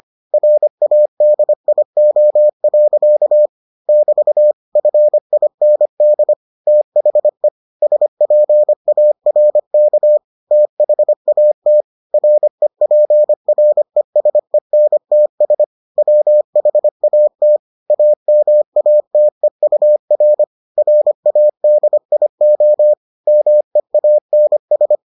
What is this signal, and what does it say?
Morse code, heard as a keyed tone, about 25 wpm, 600 Hz. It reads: RADIO. <BT> FIND THE SPARK THAT REPRESENTS WHAT AMATEUR RADIO MEANS